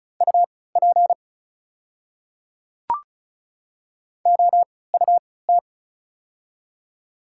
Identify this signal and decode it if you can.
Morse code, heard as a keyed tone, about 36 wpm, spaced as for 20 wpm, 690 Hz. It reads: UP OUT
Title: 3x Character Spacing - Top 100 Words - 35wpm